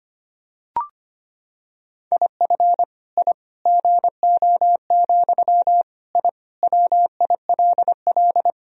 Morse code, heard as a keyed tone, about 25 wpm, 700 Hz. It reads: IF I GO, I WILL
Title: Sentences from Top 1000 Words - Review Part 3 - 25wpm